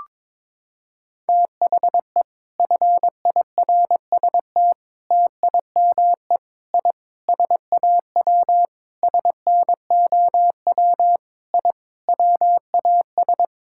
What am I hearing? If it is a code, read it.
Morse code, heard as a keyed tone, about 22 wpm, 700 Hz. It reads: THE FIRST TIME I SAW SNOW I WAS
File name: practice / 22wpm / Sentences with 4-Word Phrases - 22wpm